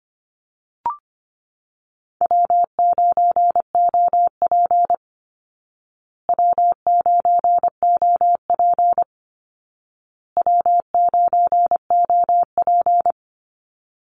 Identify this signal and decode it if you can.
Morse code, heard as a keyed tone, about 25 wpm, 700 Hz. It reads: W9OP W9OP W9OP
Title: Call Signs Repeated 2x - 25wpm